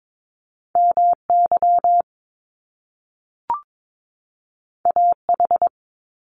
Morse code, heard as a keyed tone, about 22 wpm, 700 Hz. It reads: MY AH
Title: Top Two Letter Combinations in Words 22wpm